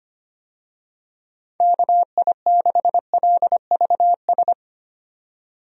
Morse code, heard as a keyed tone, about 25 wpm, 700 Hz. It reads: KI6LVS
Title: USA Call Signs - 6 Characters - 25wpm